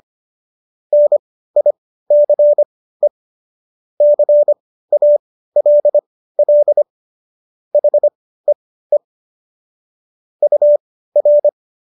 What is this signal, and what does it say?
Morse code, heard as a keyed tone, about 25 wpm, spaced as for 15 wpm, 590 Hz. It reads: NICE CALL HEE UR